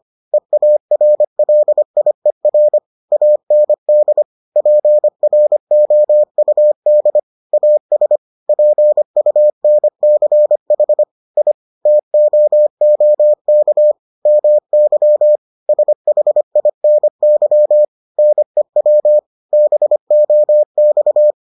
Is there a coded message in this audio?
Yes: Morse code from 0.3 to 21.3 s, heard as a keyed tone, about 25 wpm, 600 Hz. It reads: EARLIER AND PROUD AS PUNCH I TOOK MY SHINY NEW BOX